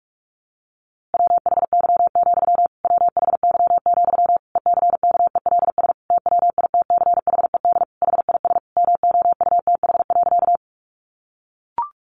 Morse code, heard as a keyed tone, about 44 wpm, 700 Hz. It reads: WHY, WHY, ECKELS TWITCHED HIS MOUTH.
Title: A Sound Of Thunder 45wpm